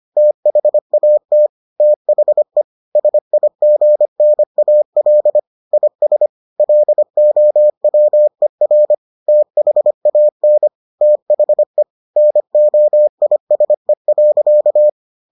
Morse code, heard as a keyed tone, about 25 wpm, 600 Hz. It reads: THAT THE SIGNAL IS LOWER THAN THE NOISE.